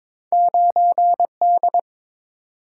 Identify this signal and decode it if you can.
Morse code, heard as a keyed tone, about 22 wpm, 700 Hz. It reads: 9D
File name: practice / 22wpm / Two Letter-Number Combinations - 22wpm